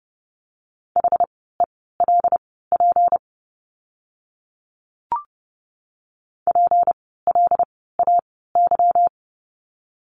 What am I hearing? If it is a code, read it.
Morse code, heard as a keyed tone, about 30 wpm, spaced as for 17 wpm, 700 Hz. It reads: HELP PLAY
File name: course / 30:10wpm / 237 Lesson 38f - Letter q - Words 30wpm